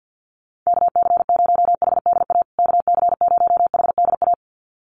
Morse code, heard as a keyed tone, about 50 wpm, 700 Hz. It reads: KC0HDA KC0HDA
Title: USA Call Signs Repeated 2x 50wpm